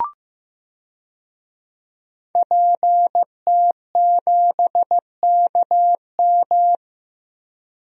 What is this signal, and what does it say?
Morse code, heard as a keyed tone, about 15 wpm, 700 Hz. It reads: PT7KM
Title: DXCC-100 Call Signs - 5 Characters - Repeated 2x - 15wpm